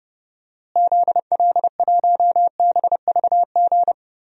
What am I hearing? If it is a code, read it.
Morse code, heard as a keyed tone, about 30 wpm, 700 Hz. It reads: ZL1BVG
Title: DXCC-100 Call Signs - 6 Characters - Repeated 3x - 30wpm